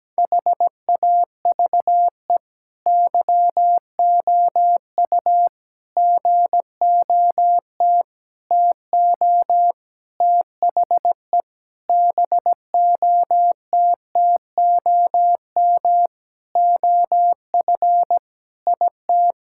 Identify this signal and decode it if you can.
Morse code, heard as a keyed tone, about 17 wpm, 700 Hz. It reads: HAVE YOU GOT TO THE BOTTOM OF IT
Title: Sentences from Top 1000 Words - Review Part 6 - 17wpm